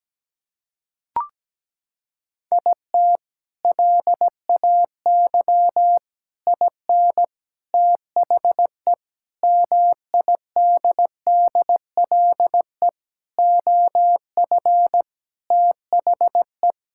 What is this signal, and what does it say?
Morse code, heard as a keyed tone, about 17 wpm, 700 Hz. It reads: IT LAY IN THE MIDDLE OF THE
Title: Sentences from Top 1000 Words - Review Part 6 - 17wpm